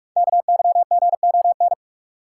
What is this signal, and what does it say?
Morse code, heard as a keyed tone, about 44 wpm, 700 Hz. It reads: KYGON
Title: Code Groups - 5 Letters - 45wpm